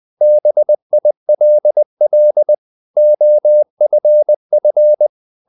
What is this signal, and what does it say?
Morse code, heard as a keyed tone, about 20 wpm, 600 Hz. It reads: BILL OFF
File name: the old man and the sea part 2 20wpm